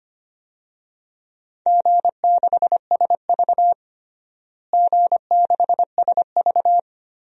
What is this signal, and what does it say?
Morse code, heard as a keyed tone, about 25 wpm, 700 Hz. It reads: G6SV G6SV
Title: DXCC-100 Call Signs - 4 Characters - Repeated 2x - 25wpm